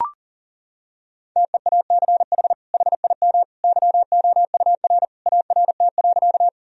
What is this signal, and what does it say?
Morse code, heard as a keyed tone, about 37 wpm, 700 Hz. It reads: TEACH HIM YOUR ART.
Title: Sentences from Top 1000 Words - Review Part 3 - 40wpm